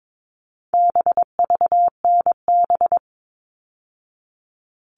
Morse code, heard as a keyed tone, about 22 wpm, 700 Hz. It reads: BVNB
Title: All Cut Numbers - 4-Digit 22wpm